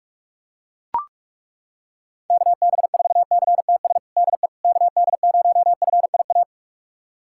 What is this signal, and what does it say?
Morse code, heard as a keyed tone, about 41 wpm, 700 Hz. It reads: KB4CTS DE KD0FIA
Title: USA Callsign DE Callsign 45wpm